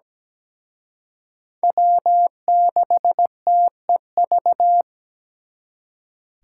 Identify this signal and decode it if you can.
Morse code, heard as a keyed tone, about 20 wpm, 700 Hz. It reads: W6TEV
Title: Speed Racing - USA 1x3 Call Signs 17wpm